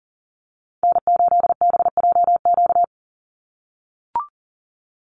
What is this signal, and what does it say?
Morse code, heard as a keyed tone, about 40 wpm, 700 Hz. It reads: N8BJQ
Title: Call Signs - 40wpm